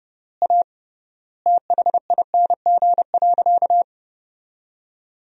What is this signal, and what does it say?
Morse code, heard as a keyed tone, about 30 wpm, 700 Hz. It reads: A THING.
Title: 3x Word-Spacing - Sentences from Top 200 Words - 30wpm